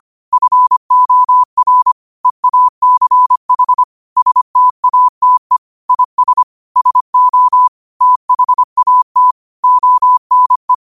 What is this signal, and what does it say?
Morse code, heard as a keyed tone, about 25 wpm, 1000 Hz. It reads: FOR EACH STATE IS SO THAT ONE
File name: qotdfiles / 25wpm